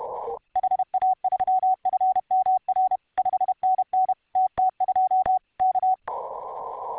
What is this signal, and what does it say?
Morse code, heard as a keyed tone, about 31 wpm, 750 Hz. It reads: HA3FMR 5NN TT2 K